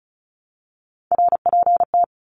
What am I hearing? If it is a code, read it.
Morse code, heard as a keyed tone, about 34 wpm, 700 Hz. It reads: RPT
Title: Abbreviations 35wpm